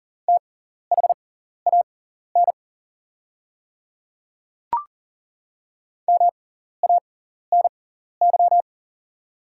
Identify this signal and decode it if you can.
Morse code, heard as a keyed tone, about 40 wpm, spaced as for 14 wpm, 700 Hz. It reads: THAN MANY
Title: 6x Character Spacing - Top 200 Words - 40wpm